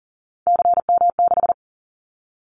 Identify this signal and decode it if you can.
Morse code, heard as a keyed tone, about 40 wpm, 690 Hz. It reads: CM6
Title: Callsign Prefixes for Top 100 DXCC - 40wpm